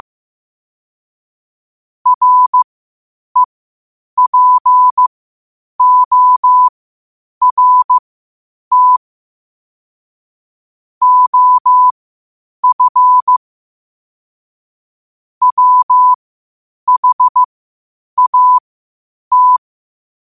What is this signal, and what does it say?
Morse code, heard as a keyed tone, about 15 wpm, spaced as for 8 wpm, 1000 Hz. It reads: REPORT OF WHAT
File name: qotdfiles / 8wpm